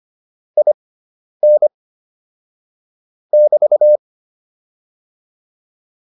Morse code, heard as a keyed tone, about 25 wpm, spaced as for 10 wpm, 600 Hz. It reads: IN <BT>